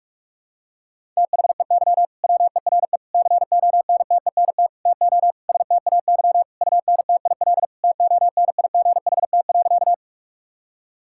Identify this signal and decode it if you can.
Morse code, heard as a keyed tone, about 41 wpm, 700 Hz. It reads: THEY WERE CONTENT TO STAY UNTIL TONIGHT.
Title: Sentences from Top 1000 Words - Review Part 6 - 45wpm